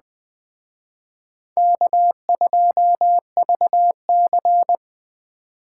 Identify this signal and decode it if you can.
Morse code, heard as a keyed tone, about 21 wpm, 700 Hz. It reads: K2VC